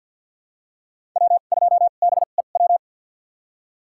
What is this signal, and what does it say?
Morse code, heard as a keyed tone, about 47 wpm, spaced as for 35 wpm, 700 Hz. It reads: W2BEW
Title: USA Call Signs - 5 Characters - 2x Character Spacing - 50wpm